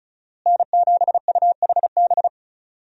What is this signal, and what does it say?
Morse code, heard as a keyed tone, about 34 wpm, 700 Hz. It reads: N7UHB
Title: USA Call Signs - 5 Characters - Repeated 3x - 35wpm